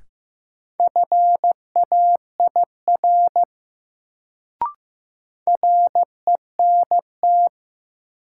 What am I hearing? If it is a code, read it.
Morse code, heard as a keyed tone, about 15 wpm, 700 Hz. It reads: FAIR RENT
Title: Sentences from Top 1000 Words - Review Part 5 - New Word Drill - 15wpm